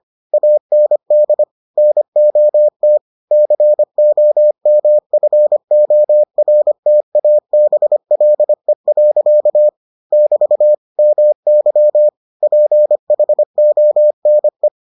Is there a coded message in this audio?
Yes: Morse code from 0.3 to 14.7 s, heard as a keyed tone, about 25 wpm, 600 Hz. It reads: AND NOT COMFORTABLE. <BT> MY PHONE